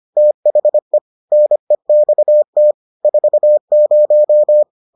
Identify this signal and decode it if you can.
Morse code, heard as a keyed tone, about 25 wpm, 600 Hz. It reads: THE NEXT 40